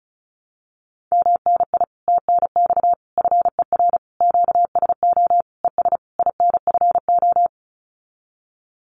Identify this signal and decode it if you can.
Morse code, heard as a keyed tone, about 34 wpm, 700 Hz. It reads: MNI TNX FER QSO ES INFO
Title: Common QSO Phrases Encore 35wpm